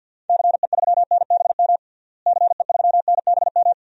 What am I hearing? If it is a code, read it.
Morse code, heard as a keyed tone, about 50 wpm, 700 Hz. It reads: CE3NBM CE3NBM
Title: DXCC-100 Call Signs - 6 Characters - Repeated 3x - 50wpm